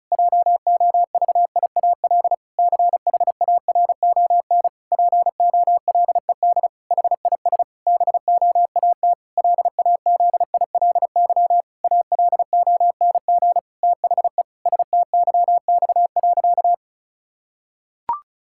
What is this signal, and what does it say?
Morse code, heard as a keyed tone, about 36 wpm, 700 Hz. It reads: JOVIAL CHARON POLED HIS BOAT LAZILY ALONG THE STYX.
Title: Pangram Challenge v2 - 35wpm